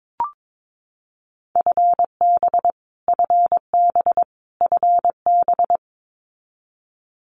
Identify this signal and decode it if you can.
Morse code, heard as a keyed tone, about 22 wpm, 700 Hz. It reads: FB FB FB
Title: Abbreviations Repeated 3x - 22wpm